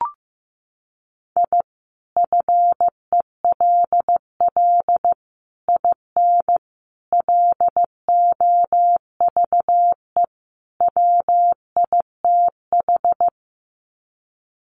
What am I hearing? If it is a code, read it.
Morse code, heard as a keyed tone, about 15 wpm, 700 Hz. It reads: I FELL IN LOVE WITH
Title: Sets of 5 Words Encore 15wpm